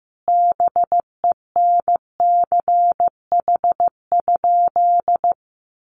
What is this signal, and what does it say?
Morse code, heard as a keyed tone, about 15 wpm, 700 Hz. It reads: BENCH?
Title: Sentences from Top 1400 Words 15wpm